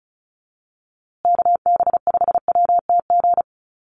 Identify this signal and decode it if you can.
Morse code, heard as a keyed tone, about 34 wpm, 700 Hz. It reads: KB5WTG
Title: USA Call Signs Repeated 3x 35wpm